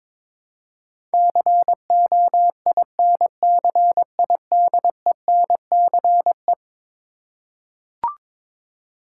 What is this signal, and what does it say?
Morse code, heard as a keyed tone, about 22 wpm, 700 Hz. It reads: COINCIDENCE
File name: practice / 22wpm / Eleven Letter Words - 22wpm